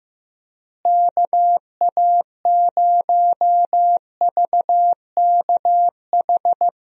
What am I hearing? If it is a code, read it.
Morse code, heard as a keyed tone, about 15 wpm, 700 Hz. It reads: KA0VKH
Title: USA Call Signs - 6 Characters - Repeated 3x - 15wpm